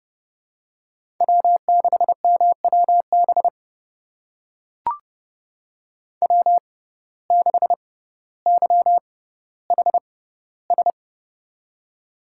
Morse code, heard as a keyed tone, about 32 wpm, spaced as for 16 wpm, 700 Hz. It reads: W6MWB W6YHS
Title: USA Call Signs - 5 Characters - 6x Character Spacing - 30wpm